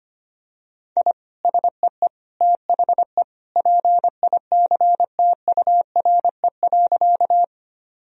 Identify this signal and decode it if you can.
Morse code, heard as a keyed tone, about 25 wpm, 700 Hz. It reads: I SEE THE PICTURE.
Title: Sentences from Top 1000 Words - Review Part 1 - 25wpm